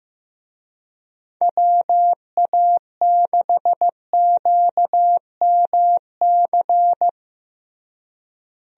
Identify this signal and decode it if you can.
Morse code, heard as a keyed tone, about 15 wpm, 700 Hz. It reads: WA6QMC
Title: USA Call Signs - 6 Characters - Repeated 2x - 15wpm